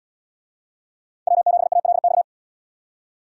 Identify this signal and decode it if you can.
Morse code, heard as a keyed tone, about 69 wpm, 700 Hz. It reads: W7ALX